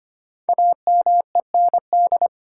Morse code, heard as a keyed tone, about 25 wpm, 700 Hz. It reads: AMEND
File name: 5 Letter Words - First 3 Letters Given - 25wpm